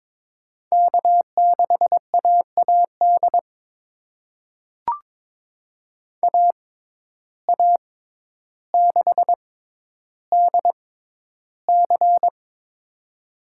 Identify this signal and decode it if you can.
Morse code, heard as a keyed tone, about 22 wpm, spaced as for 11 wpm, 700 Hz. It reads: K6AAD AA6DC
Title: USA Call Signs - 6x Character Spacing - 22wpm